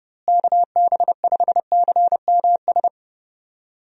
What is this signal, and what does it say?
Morse code, heard as a keyed tone, about 30 wpm, 700 Hz. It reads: KB5CMS
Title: USA Call Signs - 6 Characters - Repeated 3x - 30wpm